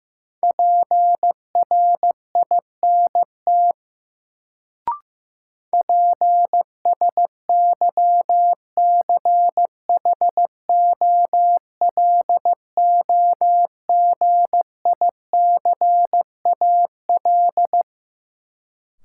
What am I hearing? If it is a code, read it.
Morse code, heard as a keyed tone, about 15 wpm, 700 Hz. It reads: PRINT PSYCHOLOGICAL
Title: Sentences from Top 1400 Words - New Word Drill - 15wpm